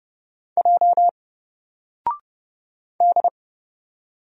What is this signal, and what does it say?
Morse code, heard as a keyed tone, about 31 wpm, 700 Hz. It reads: J D
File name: Single-Letter-Number-30wpm